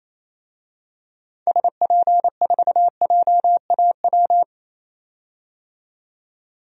Morse code, heard as a keyed tone, about 28 wpm, 700 Hz. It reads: SP4JAW
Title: DXCC-100 Call Signs - 6 Characters - 28wpm